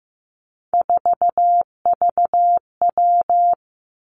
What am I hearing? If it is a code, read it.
Morse code, heard as a keyed tone, about 16 wpm, 700 Hz. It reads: 4VW
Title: Three Challenging Letter-Number Combinations 15wpm